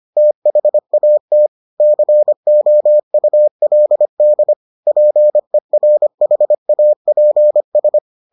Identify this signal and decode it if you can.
Morse code, heard as a keyed tone, about 25 wpm, 600 Hz. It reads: THAT COULD PERHAPS